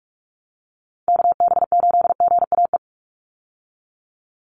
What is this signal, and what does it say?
Morse code, heard as a keyed tone, about 44 wpm, 700 Hz. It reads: KB8GAE
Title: Call Signs Repeated 3x - 45wpm